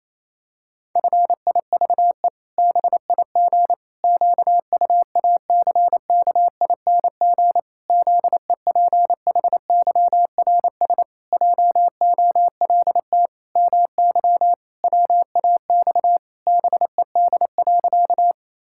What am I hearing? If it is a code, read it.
Morse code, heard as a keyed tone, about 28 wpm, 700 Hz. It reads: FIVE BIG QUACKING ZEPHYRS JOLT MY WAX BED.
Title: Pangram Challenge v2 - 28wpm